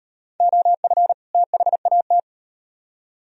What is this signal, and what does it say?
Morse code, heard as a keyed tone, about 38 wpm, 700 Hz. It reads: OF THAT